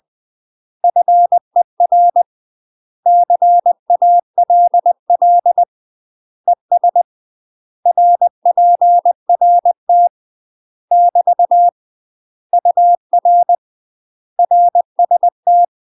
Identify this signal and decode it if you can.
Morse code, heard as a keyed tone, about 20 wpm, 700 Hz. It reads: FER CALL ES RPRT <BT> UR RST
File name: UK MASTER QSOs TRAINING / 20WPM / UK MASTER QSOs 0069